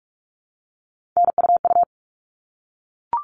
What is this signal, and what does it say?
Morse code, heard as a keyed tone, about 44 wpm, 700 Hz. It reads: NUU
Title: All Cut Numbers - 3-Digit 45wpm